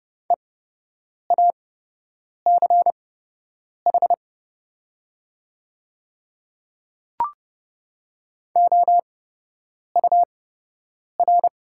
Morse code, heard as a keyed tone, about 30 wpm, spaced as for 8 wpm, 700 Hz. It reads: EACH OUR
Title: 8x Character Spacing - Top 300 Words - 30wpm